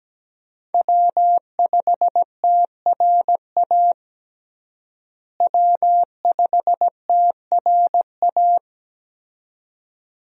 Morse code, heard as a keyed tone, about 17 wpm, 700 Hz. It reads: W5TRA W5TRA
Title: USA Call Signs - 5 Characters - Repeated 2x - 17wpm